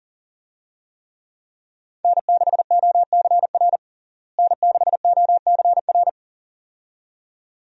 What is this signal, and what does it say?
Morse code, heard as a keyed tone, about 40 wpm, 700 Hz. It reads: N6OCR N6OCR
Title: USA Call Signs - 5 Characters - Repeated 2x - 40wpm